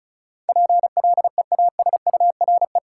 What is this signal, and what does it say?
Morse code, heard as a keyed tone, about 36 wpm, 700 Hz. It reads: PLEASURE
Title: Sentences from Top 1600 Words - New Word Drill Alt - 35wpm